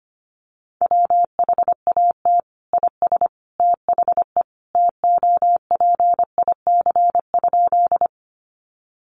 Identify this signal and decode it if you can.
Morse code, heard as a keyed tone, about 25 wpm, 700 Hz. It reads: WHAT IS THE TOPIC?
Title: Sentences from Top 1200 Words 25wpm